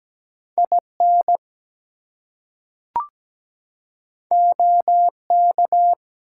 Morse code, heard as a keyed tone, about 17 wpm, 700 Hz. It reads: IN OK
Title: Two Letter Words - 17wpm